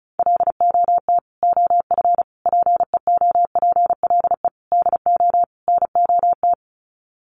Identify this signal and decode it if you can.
Morse code, heard as a keyed tone, about 34 wpm, 700 Hz. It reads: LOT OF PEOPLE DO NOT
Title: Sets of 5 Words Encore 35wpm